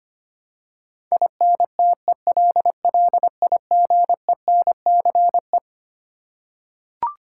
Morse code, heard as a keyed tone, about 25 wpm, 700 Hz. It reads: INTELLIGENCE
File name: Twelve Letter Words - 25wpm